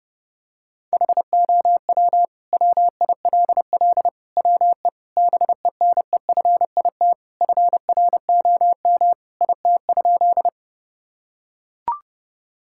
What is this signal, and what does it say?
Morse code, heard as a keyed tone, about 30 wpm, 700 Hz. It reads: HOW WILL WE BENEFIT FROM IT?
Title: Sentences from Top 1000 Words - Review Part 6 - 30wpm